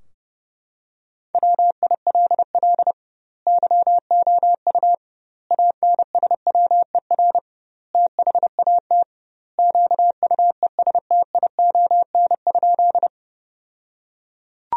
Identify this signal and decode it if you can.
Morse code, heard as a keyed tone, about 30 wpm, 700 Hz. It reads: WILL YOU ANSWER THAT QUESTION?
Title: 2x Word-Spacing - Sentences from Top 400 Words - 30wpm